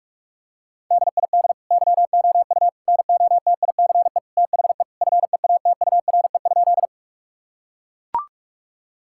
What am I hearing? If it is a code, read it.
Morse code, heard as a keyed tone, about 41 wpm, 700 Hz. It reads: DID YOU NOTICE THE FEATURE?
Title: Sentences from Top 1000 Words - Review Part 5 - 45wpm